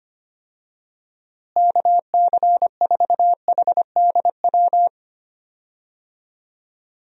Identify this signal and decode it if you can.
Morse code, heard as a keyed tone, about 25 wpm, 700 Hz. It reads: KC4HDW
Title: USA Call Signs - 6 Characters - Repeated 2x - 25wpm